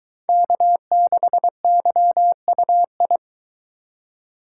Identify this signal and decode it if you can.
Morse code, heard as a keyed tone, about 23 wpm, 700 Hz. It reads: K6YUI